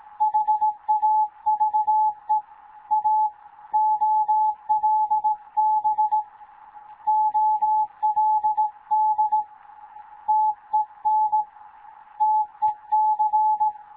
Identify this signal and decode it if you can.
Morse code, heard as a keyed tone, about 17 wpm, 860 Hz. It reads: HAVE A OLB OLD TEN TEC
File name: Morze-HA2MN / hangp / 2 11 GYAKORLAT